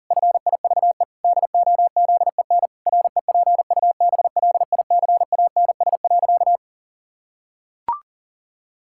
Morse code, heard as a keyed tone, about 40 wpm, 700 Hz. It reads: FIVE DOZEN REPUBLICANS.
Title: Pangram Challenge v2 - 40wpm